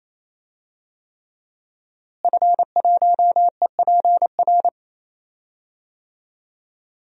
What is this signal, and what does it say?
Morse code, heard as a keyed tone, about 28 wpm, 700 Hz. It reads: F1EPR
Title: DXCC-100 Call Signs - 5 Characters - Repeated 3x - 28wpm